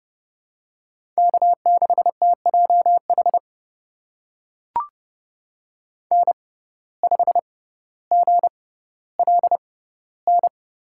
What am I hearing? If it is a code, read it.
Morse code, heard as a keyed tone, about 30 wpm, spaced as for 15 wpm, 700 Hz. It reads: K6TJH N5GLN
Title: USA Call Signs - 5 Characters - 6x Character Spacing - 30wpm